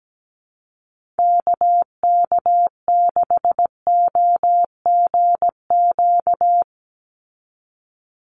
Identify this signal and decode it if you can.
Morse code, heard as a keyed tone, about 17 wpm, 700 Hz. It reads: KK6OGQ
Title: USA Call Signs 17wpm